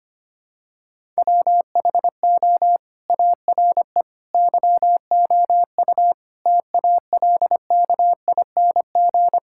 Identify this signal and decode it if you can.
Morse code, heard as a keyed tone, about 25 wpm, 700 Hz. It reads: WHO ARE YOU TALKING
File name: practice / 25wpm / Sentences with 2-Word Phrases - 25wpm